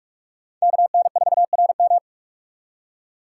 Morse code, heard as a keyed tone, about 46 wpm, 700 Hz. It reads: KN4RM
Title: USA Call Signs - 5 Characters - 4x Character Spacing - 45wpm